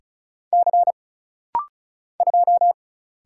Morse code, heard as a keyed tone, about 36 wpm, 700 Hz. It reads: C 2
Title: Single Letter-Number Alt - Rapid-Fire 35wpm